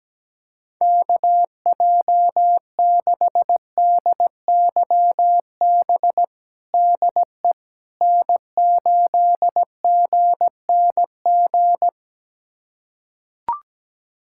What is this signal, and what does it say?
Morse code, heard as a keyed tone, about 17 wpm, 700 Hz. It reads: KJ6DYB DE N8GNG
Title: USA Callsign DE Callsign 17wpm